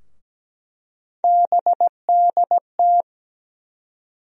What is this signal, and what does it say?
Morse code, heard as a keyed tone, about 17 wpm, 700 Hz. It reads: BDT
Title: USA Call Sign Postfixes - 3 Letters 17wpm